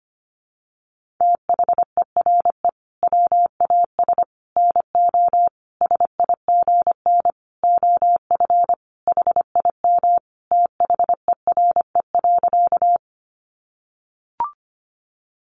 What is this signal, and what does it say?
Morse code, heard as a keyed tone, about 25 wpm, 700 Hz. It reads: THERE WAS NO SIGN OF HIM THERE.
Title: Sentences from Top 1000 Words 25wpm